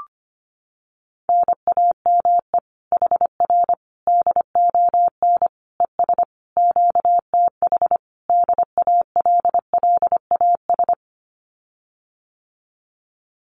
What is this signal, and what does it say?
Morse code, heard as a keyed tone, about 25 wpm, 700 Hz. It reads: NAME HR DON ES QTH DALLAS
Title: Common QSO Phrases Encore 25wpm